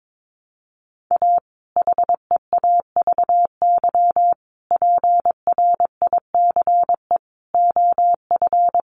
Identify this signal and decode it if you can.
Morse code, heard as a keyed tone, about 22 wpm, 700 Hz. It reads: A HEAVY PRICE OF
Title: Sentences from Top 1100 Words 22wpm